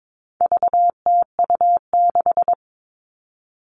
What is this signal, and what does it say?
Morse code, heard as a keyed tone, about 22 wpm, 700 Hz. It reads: VTU6
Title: All Cut Numbers - 4-Digit 22wpm